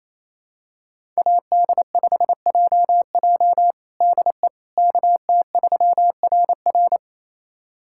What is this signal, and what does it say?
Morse code, heard as a keyed tone, about 27 wpm, 700 Hz. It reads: AD5JJ DE KT3RR
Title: USA Callsign DE Callsign - 5 Characters - 28wpm